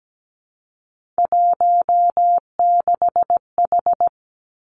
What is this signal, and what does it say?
Morse code, heard as a keyed tone, about 17 wpm, 700 Hz. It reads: 16H
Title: Three Challenging Letter-Number Combinations 17wpm